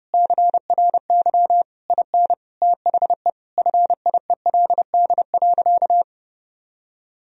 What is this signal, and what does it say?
Morse code, heard as a keyed tone, about 30 wpm, 700 Hz. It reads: CRY IN THE FIELD.
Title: Sentences from Top 1000 Words - Review Part 2 - 30wpm